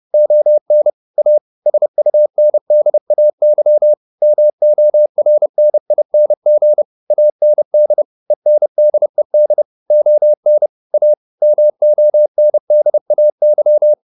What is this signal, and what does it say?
Morse code, heard as a keyed tone, about 30 wpm, 600 Hz. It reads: ON A SUNDAY MORNING AND ENDED ON A MONDAY